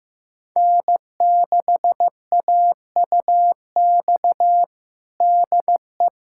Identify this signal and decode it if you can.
Morse code, heard as a keyed tone, about 15 wpm, 700 Hz. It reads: N6AUX DE
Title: USA Callsign DE Callsign - 5 Characters - 15wpm